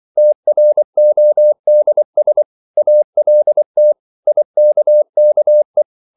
Morse code, heard as a keyed tone, about 24 wpm, 600 Hz. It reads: TRODS ALT IKKE
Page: Ferie 24 wpm